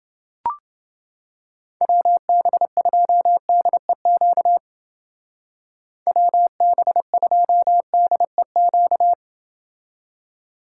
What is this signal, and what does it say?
Morse code, heard as a keyed tone, about 29 wpm, 700 Hz. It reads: WB2DEQ WB2DEQ